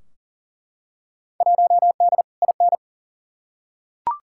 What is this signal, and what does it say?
Morse code, heard as a keyed tone, about 40 wpm, 700 Hz. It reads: 1D IN
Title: Field Day Exchanges - 40wpm